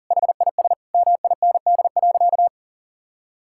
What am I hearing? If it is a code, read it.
Morse code, heard as a keyed tone, about 40 wpm, 700 Hz. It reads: HIS MIND.
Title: Sentences from Top 1000 Words - Review Part 5 - 40wpm